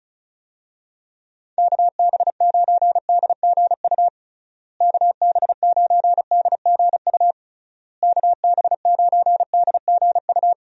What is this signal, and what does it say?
Morse code, heard as a keyed tone, about 34 wpm, 700 Hz. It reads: KB9DGU KB9DGU KB9DGU
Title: USA Call Signs - 6 Characters - Repeated 3x - 35wpm